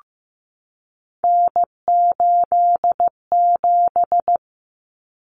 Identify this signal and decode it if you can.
Morse code, heard as a keyed tone, about 15 wpm, 700 Hz. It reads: N87
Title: Common Cut Numbers - 3-Digit 15wpm